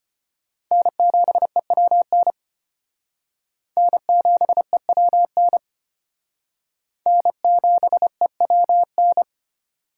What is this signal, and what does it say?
Morse code, heard as a keyed tone, about 29 wpm, 700 Hz. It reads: N7EWN N7EWN N7EWN
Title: Speed Racing - USA 1x3 Call Signs 25wpm